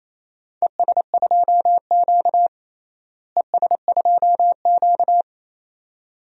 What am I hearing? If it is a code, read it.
Morse code, heard as a keyed tone, about 28 wpm, 700 Hz. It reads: ES2Q ES2Q
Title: DXCC-100 Call Signs - 4 Characters - Repeated 2x - 28wpm